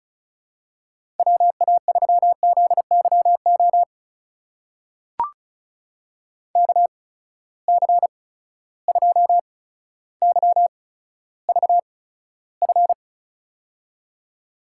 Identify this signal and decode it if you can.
Morse code, heard as a keyed tone, about 36 wpm, spaced as for 15 wpm, 700 Hz. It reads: WA3ZYO KC2YVF
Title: USA Call Signs - 6 Characters - 8x Character Spacing - 35wpm